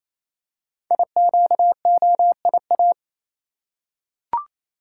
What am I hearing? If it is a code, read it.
Morse code, heard as a keyed tone, about 29 wpm, 700 Hz. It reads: IQOIA
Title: Code Groups - 5 Letters - 28wpm